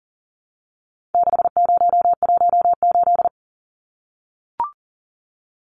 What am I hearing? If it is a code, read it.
Morse code, heard as a keyed tone, about 40 wpm, 700 Hz. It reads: 6018
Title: Numbers - 4-Digit 40wpm